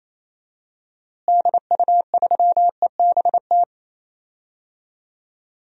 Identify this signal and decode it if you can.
Morse code, heard as a keyed tone, about 28 wpm, 700 Hz. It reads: DU3EBT
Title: DXCC-100 Call Signs - 6 Characters - Repeated 2x - 28wpm